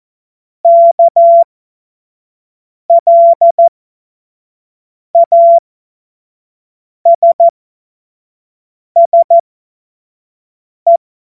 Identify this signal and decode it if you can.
Morse code, heard as a keyed tone, about 14 wpm, spaced as for 5 wpm, 680 Hz. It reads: KLASSE
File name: Morsekurs / Witze / Witz 25. April 2023 25 F70